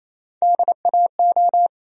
Morse code, heard as a keyed tone, about 28 wpm, 700 Hz. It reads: DAO
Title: USA Call Sign Postfixes - 3 Letters 28wpm